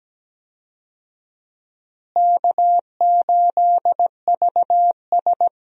Morse code, heard as a keyed tone, about 17 wpm, 700 Hz. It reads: K8VS
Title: USA Call Signs - 4 Characters - Repeated 3x - 17wpm